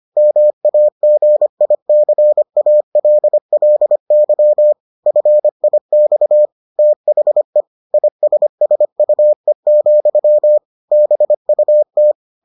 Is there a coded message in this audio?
Yes: Morse code from 0.2 to 12.1 s, heard as a keyed tone, about 25 wpm, 600 Hz. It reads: MAGICALLY FIX THE ISSUE, BUT